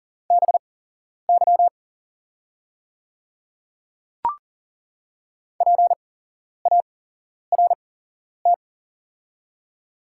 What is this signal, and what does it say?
Morse code, heard as a keyed tone, about 38 wpm, spaced as for 11 wpm, 690 Hz. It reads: BY PART
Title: 8x Character Spacing - Top 200 Words - 40wpm